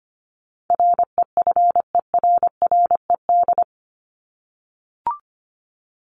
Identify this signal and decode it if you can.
Morse code, heard as a keyed tone, about 25 wpm, 700 Hz. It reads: REFERRED
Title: ED Suffix - Top 100 Words with Base Words 25wpm